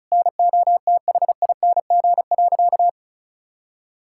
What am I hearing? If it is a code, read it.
Morse code, heard as a keyed tone, about 36 wpm, 700 Hz. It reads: NOTHING.
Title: Sentences from Top 1000 Words - Review Part 5 - 35wpm